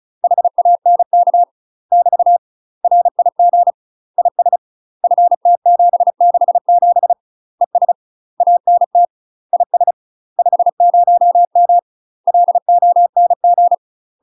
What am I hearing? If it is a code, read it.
Morse code, heard as a keyed tone, about 35 wpm, 700 Hz. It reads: HANK <BT> RIG IS FT767 ES ANT IS 50M LONG
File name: UK MASTER QSOs TRAINING / 35WPM / UK MASTER QSOs 0026